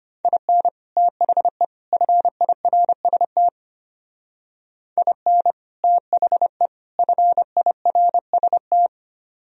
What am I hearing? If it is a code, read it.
Morse code, heard as a keyed tone, about 31 wpm, 700 Hz. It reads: IN THE FIRST IN THE FIRST
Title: Speed-Racing - Sets of 3 Words 25wpm